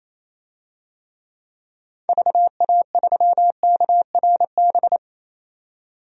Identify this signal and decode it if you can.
Morse code, heard as a keyed tone, about 28 wpm, 700 Hz. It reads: VA3KRB
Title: DXCC-100 Call Signs - 6 Characters - 28wpm